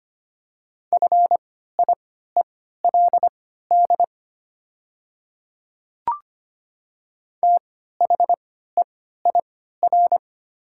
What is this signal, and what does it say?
Morse code, heard as a keyed tone, about 25 wpm, spaced as for 14 wpm, 700 Hz. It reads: FIELD THEIR
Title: 3x Character Spacing - Top 500 Words - 25wpm